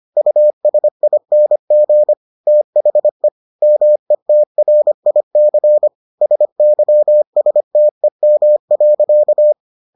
Morse code, heard as a keyed tone, about 25 wpm, 600 Hz. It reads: USING THE METRIC SYSTEM.